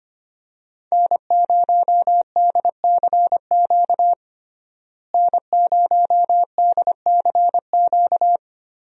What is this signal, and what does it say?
Morse code, heard as a keyed tone, about 25 wpm, 700 Hz. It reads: N0DCQ N0DCQ
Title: USA Call Signs - 5 Characters - Repeated 3x - 25wpm